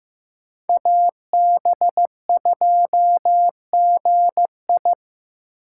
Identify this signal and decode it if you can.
Morse code, heard as a keyed tone, about 15 wpm, 700 Hz. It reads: AB2GI
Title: K1USN SST QSO Practice - 15wpm